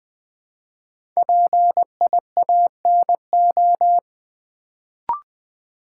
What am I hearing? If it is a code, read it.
Morse code, heard as a keyed tone, about 20 wpm, 700 Hz. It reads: PIANO
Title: Build A Word - 1 Letter At A Time - 5-Letter Words - 20wpm